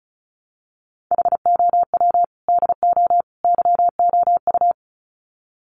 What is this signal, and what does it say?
Morse code, heard as a keyed tone, about 34 wpm, 700 Hz. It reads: HOW DO YOU
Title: Sets of 3 Words 35wpm